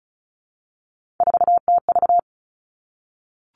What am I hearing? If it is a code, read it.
Morse code, heard as a keyed tone, about 36 wpm, 700 Hz. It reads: 4TV
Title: Three Challenging Letter-Number Combinations 35wpm